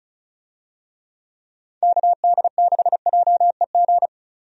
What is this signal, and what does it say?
Morse code, heard as a keyed tone, about 36 wpm, 700 Hz. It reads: KD6JEG
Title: USA Call Signs - 6 Characters - 35wpm